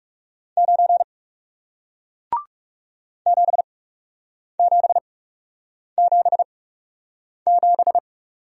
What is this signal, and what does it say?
Morse code, heard as a keyed tone, about 40 wpm, spaced as for 8 wpm, 700 Hz. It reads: 9 7777